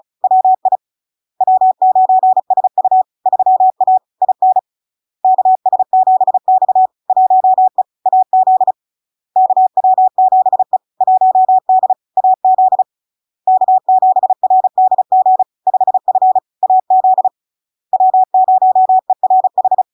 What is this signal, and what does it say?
Morse code, heard as a keyed tone, about 36 wpm, 750 Hz. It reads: WI W9SU 3A IN KS7X 1E AZ KW7E 1D AZ K7RDG 5F AZ W0ERH
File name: LARK-FD-2022-Log-in-CW-35wpm-750-Hz